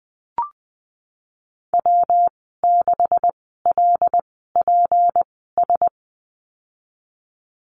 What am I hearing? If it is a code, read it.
Morse code, heard as a keyed tone, about 20 wpm, spaced as for 15 wpm, 700 Hz. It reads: W6LPS
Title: 173 Lesson 28g - Number 9 - Call Signs 20wpm